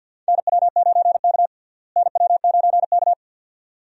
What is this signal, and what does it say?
Morse code, heard as a keyed tone, about 52 wpm, 700 Hz. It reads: NW9K NW9K
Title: USA Call Signs - 4 Characters - Repeated 2x - 50wpm